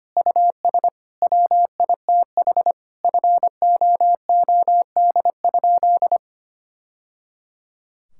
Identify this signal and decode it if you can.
Morse code, heard as a keyed tone, about 25 wpm, 700 Hz. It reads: US WITH FOOD?
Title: Sentences from Top 1000 Words - Review Part 5 - 25wpm